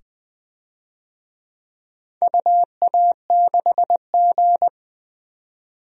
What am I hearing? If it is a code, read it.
Morse code, heard as a keyed tone, about 21 wpm, 700 Hz. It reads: UA6G
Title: DXCC-100 Call Signs - 4 Characters - 20wpm